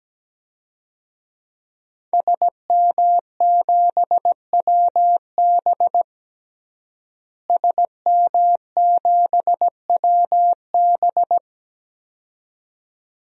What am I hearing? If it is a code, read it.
Morse code, heard as a keyed tone, about 17 wpm, 700 Hz. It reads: SM7WB SM7WB
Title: DXCC-100 Call Signs - 5 Characters - Repeated 2x - 17wpm